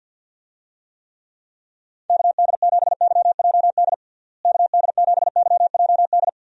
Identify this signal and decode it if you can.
Morse code, heard as a keyed tone, about 52 wpm, 700 Hz. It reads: KD7YJD KD7YJD
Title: USA Call Signs - 6 Characters - Repeated 2x - 50wpm